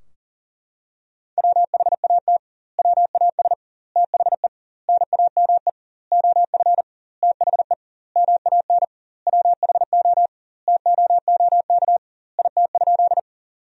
Morse code, heard as a keyed tone, about 40 wpm, 700 Hz. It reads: WHAT WAS THE NAME OF THE MAN WHO TOOK IT?
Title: 2x Word-Spacing - Sentences from Top 300 Words - 40wpm